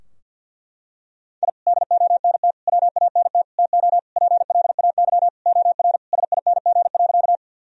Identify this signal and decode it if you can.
Morse code, heard as a keyed tone, about 49 wpm, 700 Hz. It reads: I DONT WANT TO PLAY OR SING.
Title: Sentences from Top 1000 Words - Review Part 2 - 50wpm